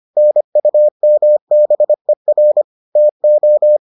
Morse code, heard as a keyed tone, about 25 wpm, 600 Hz. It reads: NUMBER TO